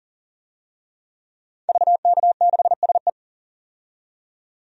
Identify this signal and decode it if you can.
Morse code, heard as a keyed tone, about 40 wpm, 700 Hz. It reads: VK6SE
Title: DXCC-100 Call Signs - 5 Characters - 40wpm